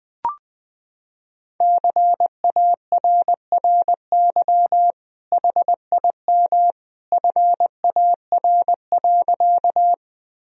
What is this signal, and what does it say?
Morse code, heard as a keyed tone, about 20 wpm, 700 Hz. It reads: CARRY HIM FAR.
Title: Sentences from Top 1000 Words - Review Part 1 - 20wpm